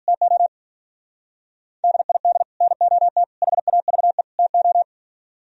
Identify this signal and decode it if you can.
Morse code, heard as a keyed tone, about 42 wpm, 700 Hz. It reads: TO DID NOT HAVE TO
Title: Speed-Racing - Sets of 4 Words 35wpm